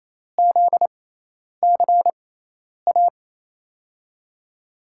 Morse code, heard as a keyed tone, about 28 wpm, spaced as for 10 wpm, 700 Hz. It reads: ZCA
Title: USA Call Signs - 5 Characters - 6x Character Spacing - 28wpm